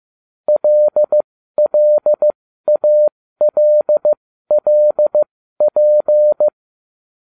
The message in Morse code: LLALLP

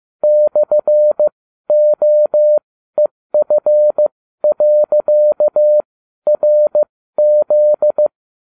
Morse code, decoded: /OEF.RZ

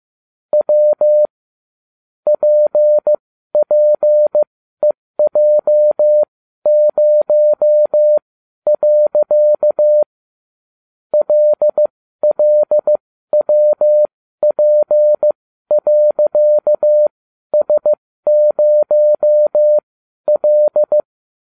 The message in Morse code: W PPEJ0. LLWP.S0L

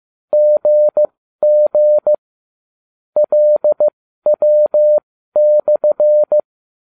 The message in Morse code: GG LW/

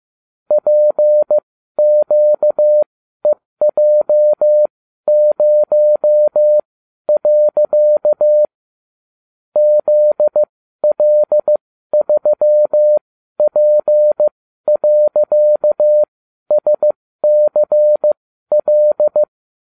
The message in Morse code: PQEJ0. ZL3P.SCL

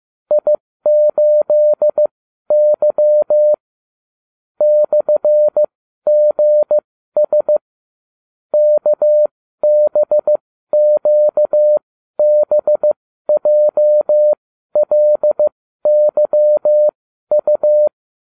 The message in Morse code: I8Y /GS KBQBJLYU

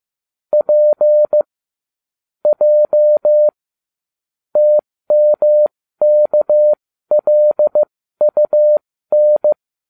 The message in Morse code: P J TMKLUN

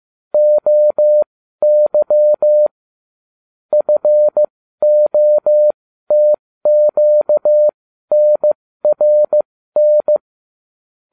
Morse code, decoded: OY FOTQNRN

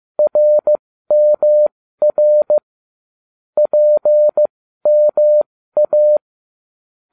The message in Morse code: RMR PMA

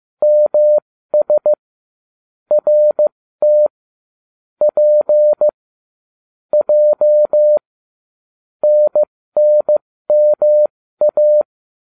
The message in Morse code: MS RT P J NNMA

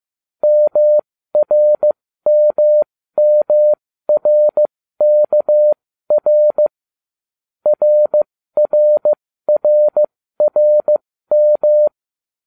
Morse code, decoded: MRMMRKR RRRRM

